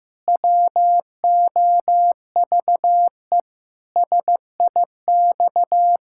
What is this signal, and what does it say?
Morse code, heard as a keyed tone, about 15 wpm, 700 Hz. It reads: WOVE SIX
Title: Pangram Challenge v2 - 15wpm